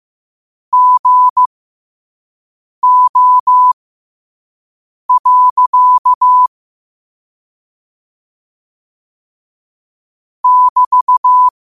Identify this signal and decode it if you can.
Morse code, heard as a keyed tone, about 15 wpm, spaced as for 5 wpm, 1000 Hz. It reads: GO. <BT>